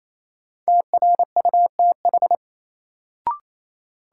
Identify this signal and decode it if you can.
Morse code, heard as a keyed tone, about 28 wpm, 700 Hz. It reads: TRUTH
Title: Sets of 5 Words - New Word Drill - 28wpm